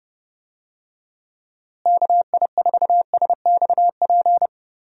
Morse code, heard as a keyed tone, about 32 wpm, 700 Hz. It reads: KI4SXP
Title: USA Call Signs - 6 Characters - Repeated 3x - 30wpm